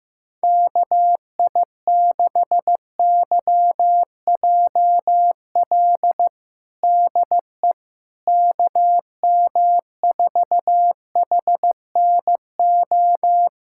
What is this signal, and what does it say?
Morse code, heard as a keyed tone, about 15 wpm, 700 Hz. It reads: KI6YJL DE KM4HNO
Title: USA Callsign DE Callsign - 6 Characters - 15wpm